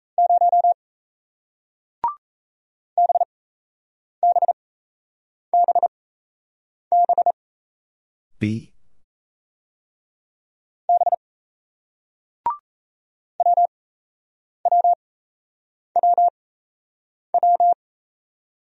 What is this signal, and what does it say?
Morse code, heard as a keyed tone, about 37 wpm, 700 Hz. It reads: 0 B B B B B W W W W